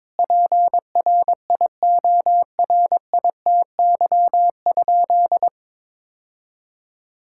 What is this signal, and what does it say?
Morse code, heard as a keyed tone, about 22 wpm, 700 Hz. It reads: PRIORITY?
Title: Sentences from Top 1600 Words 22wpm